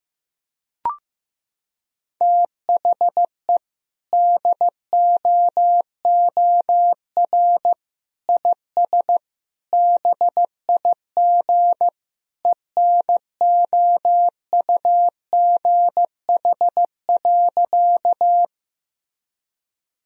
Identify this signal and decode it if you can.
Morse code, heard as a keyed tone, about 15 wpm, 700 Hz. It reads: THE DOOR IS BIG ENOUGH.
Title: Sentences from Top 1000 Words - Review Part 1 - 15wpm